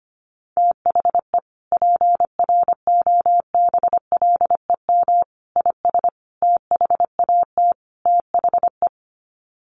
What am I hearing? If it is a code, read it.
Morse code, heard as a keyed tone, about 25 wpm, 700 Hz. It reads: THE PROBLEM IS THAT THE
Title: Sets of 5 Words Encore2 25wpm